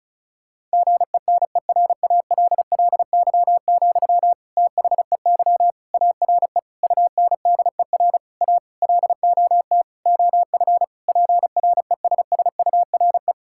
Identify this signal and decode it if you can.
Morse code, heard as a keyed tone, about 34 wpm, 700 Hz. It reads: GENERALLY, THEY ARE UNDER A LOT OF PRESSURE
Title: Sentences from Top 1000 Words - Review Part 6 - 35wpm